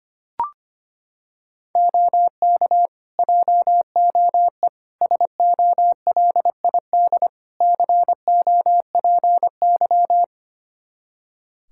Morse code, heard as a keyed tone, about 25 wpm, 700 Hz. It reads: OK JOE SOLID COPY
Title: Practice QSO - Exchange 3 - More Info - 25wpm